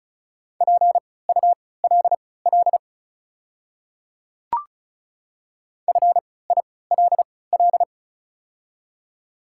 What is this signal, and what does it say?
Morse code, heard as a keyed tone, about 34 wpm, spaced as for 20 wpm, 690 Hz. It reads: PULL FILL
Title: 3x Character Spacing - Top 500 Words - 35wpm